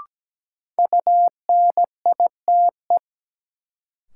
Morse code, heard as a keyed tone, about 17 wpm, 700 Hz. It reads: UNITE
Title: Sentences from Top 1000 Words - Review Part 5 - New Word Drill - 17wpm